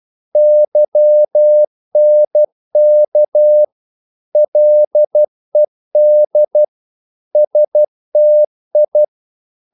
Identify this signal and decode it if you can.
Morse code, heard as a keyed tone, about 12 wpm, 600 Hz. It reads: YNK LED STI
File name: Korte ord DK 12wpm